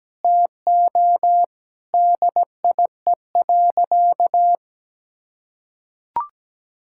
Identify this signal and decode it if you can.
Morse code, heard as a keyed tone, about 17 wpm, 700 Hz. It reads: TO DIE.
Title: Sentences from Top 1600 Words 17wpm